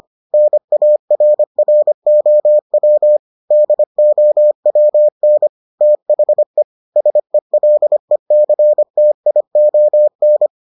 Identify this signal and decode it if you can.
Morse code, heard as a keyed tone, about 25 wpm, 600 Hz. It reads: NARROW DOWN THE SELECTION